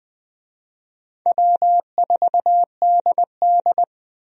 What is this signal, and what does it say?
Morse code, heard as a keyed tone, about 20 wpm, 700 Hz. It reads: W4DD
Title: USA Call Signs - 4 Characters - Repeated 3x - 20wpm